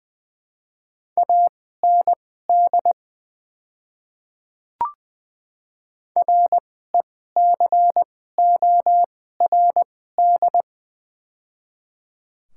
Morse code, heard as a keyed tone, about 20 wpm, spaced as for 14 wpm, 700 Hz. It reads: AND RECORD
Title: 2x Character Spacing - Top 500 Words - 20wpm